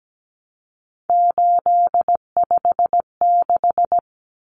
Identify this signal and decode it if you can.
Morse code, heard as a keyed tone, about 17 wpm, 700 Hz. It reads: 856
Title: Numbers - 3-Digit 17wpm